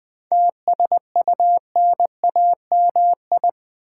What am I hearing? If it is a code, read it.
Morse code, heard as a keyed tone, about 20 wpm, 700 Hz. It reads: TSUNAMI
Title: 7 Letter Words - First 3 Letters Given - 20wpm